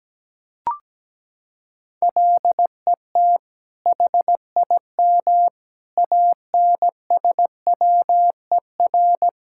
Morse code, heard as a keyed tone, about 17 wpm, 700 Hz. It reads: LET HIM ANSWER
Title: Sentences from Top 1000 Words - Review Part 5 - 17wpm